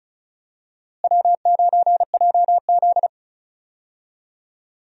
Morse code, heard as a keyed tone, about 36 wpm, 700 Hz. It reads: W9JZ
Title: USA Call Signs - 4 Characters - Repeated 2x - 35wpm